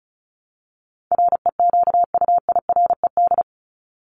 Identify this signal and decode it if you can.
Morse code, heard as a keyed tone, about 34 wpm, 700 Hz. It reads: REQUIRED
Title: ED Suffix - Top 100 Words with Base Words 35wpm